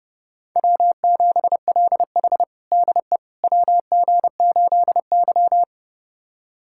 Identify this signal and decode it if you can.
Morse code, heard as a keyed tone, about 30 wpm, 700 Hz. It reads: W7LH DE WG8Y
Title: USA Callsign DE Callsign - 4 Characters - 30wpm